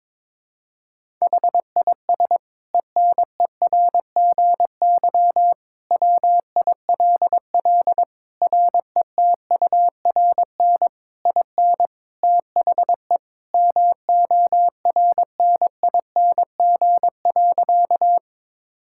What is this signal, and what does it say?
Morse code, heard as a keyed tone, about 22 wpm, 700 Hz. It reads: HIS ENERGY WILL RETURN IN THE MORNING.
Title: Sentences from Top 1000 Words - Review Part 6 - 22wpm